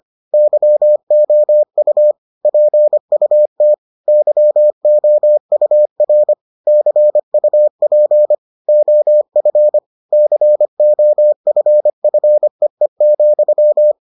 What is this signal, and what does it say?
Morse code, heard as a keyed tone, about 25 wpm, 600 Hz. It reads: YOU PUT YOUR CUP OF COFFEE,